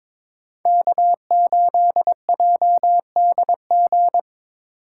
Morse code, heard as a keyed tone, about 26 wpm, 700 Hz. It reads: K8JDG
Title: Speed Racing - USA 1x3 Call Signs 22wpm